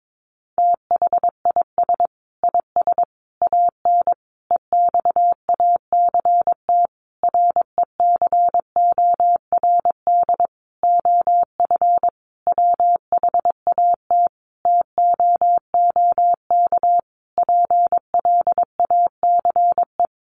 Morse code, heard as a keyed tone, about 22 wpm, 700 Hz. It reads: THIS IS AN EXACT RECORD OF WHAT TOOK PLACE
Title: Sentences from Top 1300 Words 22wpm